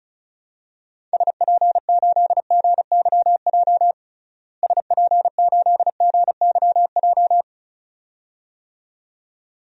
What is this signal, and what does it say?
Morse code, heard as a keyed tone, about 34 wpm, 700 Hz. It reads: SP8GYJ SP8GYJ
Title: DXCC-100 Call Signs - 6 Characters - Repeated 2x - 35wpm